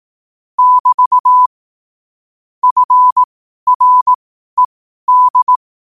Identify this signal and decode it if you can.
Morse code, heard as a keyed tone, about 18 wpm, spaced as for 12 wpm, 1000 Hz. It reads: <BT> FRED
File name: qotdfiles / 12wpm